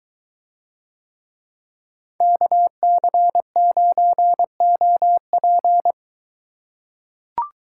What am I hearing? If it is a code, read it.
Morse code, heard as a keyed tone, about 23 wpm, 700 Hz. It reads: KC9OP